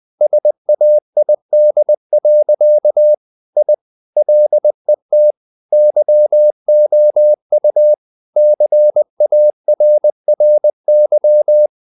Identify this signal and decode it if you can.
Morse code, heard as a keyed tone, about 20 wpm, 600 Hz. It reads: SAID. I LET YOU CARRY
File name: the old man and the sea / the old man and the sea part 1 20wpm